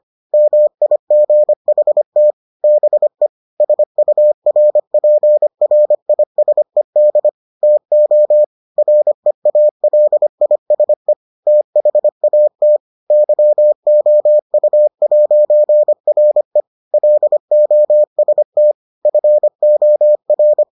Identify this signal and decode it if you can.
Morse code, heard as a keyed tone, about 25 wpm, 600 Hz. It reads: MIGHT BE SURPRISED TO REALISE THAT YOU'RE LOST FOR